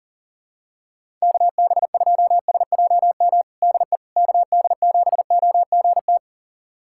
Morse code, heard as a keyed tone, about 40 wpm, 700 Hz. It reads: KB2SJM DE KD7OGT
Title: USA Callsign DE Callsign - 6 Characters - 40wpm